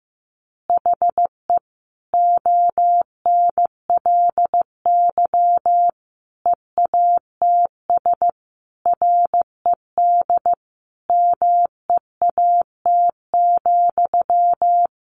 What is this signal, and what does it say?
Morse code, heard as a keyed tone, about 15 wpm, 700 Hz. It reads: HE ONLY EATS RED MEAT,
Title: Easy Phrases 15wpm